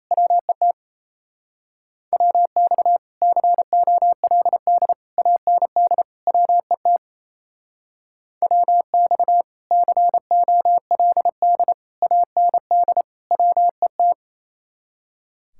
Morse code, heard as a keyed tone, about 33 wpm, 700 Hz. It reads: WET WX COLD AND WET WX COLD AND WET
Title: Speed-Racing Common QSO Phrases - 28wpm